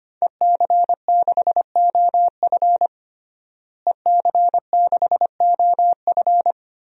Morse code, heard as a keyed tone, about 25 wpm, 700 Hz. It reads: EC6OF EC6OF
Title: DXCC-100 Call Signs - 5 Characters - Repeated 2x - 25wpm